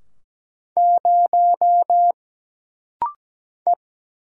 Morse code, heard as a keyed tone, about 17 wpm, 700 Hz. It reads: 0 E
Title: Single Letter-Number - Rapid-Fire 17wpm